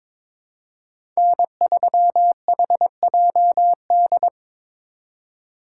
Morse code, heard as a keyed tone, about 25 wpm, 700 Hz. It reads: N3HJD
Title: Speed Racing - USA 1x3 Call Signs 22wpm